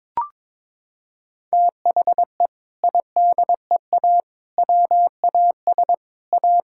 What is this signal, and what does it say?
Morse code, heard as a keyed tone, about 22 wpm, 700 Hz. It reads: THE IDEA WAS A
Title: Sentences from Top 1000 Words - Review Part 6 - 22wpm